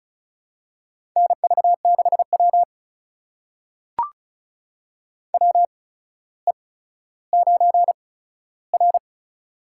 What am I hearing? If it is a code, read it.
Morse code, heard as a keyed tone, about 34 wpm, spaced as for 15 wpm, 700 Hz. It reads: NV6W WE9R
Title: USA Call Signs - 4 Characters - 8x Character Spacing - 35wpm